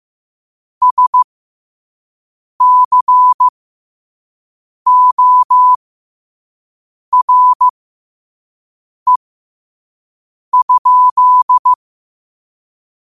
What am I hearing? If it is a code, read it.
Morse code, heard as a keyed tone, about 15 wpm, spaced as for 5 wpm, 1000 Hz. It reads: SCORE?
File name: qotdfiles / 5wpm